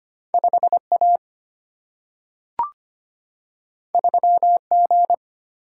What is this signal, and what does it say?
Morse code, heard as a keyed tone, about 25 wpm, 700 Hz. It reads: 5A 3G
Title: Single-Letter Single-Number 25wpm